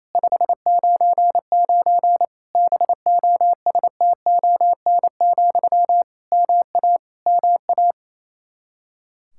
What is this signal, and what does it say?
Morse code, heard as a keyed tone, about 28 wpm, 700 Hz. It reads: 599 BOSTON, MA MA
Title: Practice QSO - Exchange 1 - Beginning - 28wpm